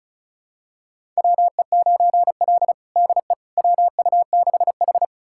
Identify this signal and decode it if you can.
Morse code, heard as a keyed tone, about 36 wpm, 700 Hz. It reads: WE9L DE WU6H
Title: USA Callsign DE Callsign - 4 Characters - 35wpm